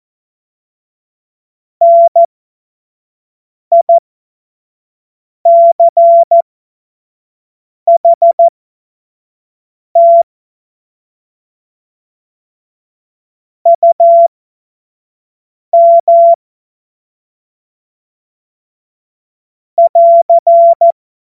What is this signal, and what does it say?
Morse code, heard as a keyed tone, about 14 wpm, spaced as for 5 wpm, 680 Hz. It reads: NICHT UM <AR>
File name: Morsekurs